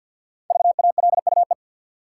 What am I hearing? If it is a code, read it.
Morse code, heard as a keyed tone, about 53 wpm, 700 Hz. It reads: VALUE
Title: Sentences from Top 600 Words - New Word Drill - 50wpm